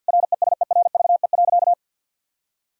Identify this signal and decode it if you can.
Morse code, heard as a keyed tone, about 48 wpm, 700 Hz. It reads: RESERVE.
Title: Sentences from Top 1000 Words - Review Part 6 - 50wpm